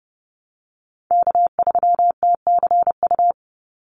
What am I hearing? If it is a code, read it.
Morse code, heard as a keyed tone, about 30 wpm, 700 Hz. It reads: K3TCU
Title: NA SOTA Chaser Callsigns 30wpm